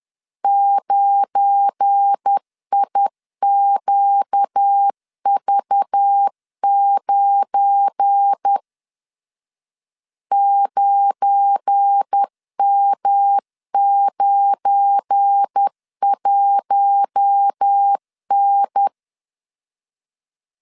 Morse code, heard as a keyed tone, about 10 wpm, 800 Hz. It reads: 9IQV9 9M91N
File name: LEZIONI / 7 GF9Z / 7 GF9 10wpm 2747